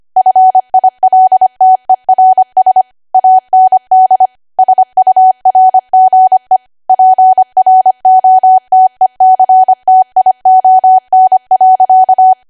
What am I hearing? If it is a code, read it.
Morse code, heard as a keyed tone, about 25 wpm, 750 Hz. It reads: FILTERS AND SURGE PROTECTION.
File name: cwa / W1AW-Level3 / 25WPM